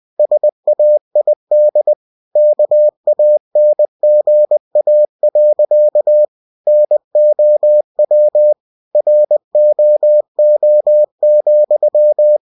Morse code, heard as a keyed tone, about 20 wpm, 600 Hz. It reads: SAID KANGA. NOW ROO,